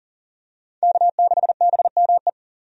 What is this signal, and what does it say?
Morse code, heard as a keyed tone, about 38 wpm, 700 Hz. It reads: K6BME
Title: USA Call Signs - 5 Characters - 40wpm